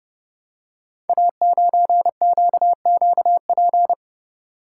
Morse code, heard as a keyed tone, about 28 wpm, 700 Hz. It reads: A9QQP
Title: Code Groups - 5 Letter:Numbers - 30wpm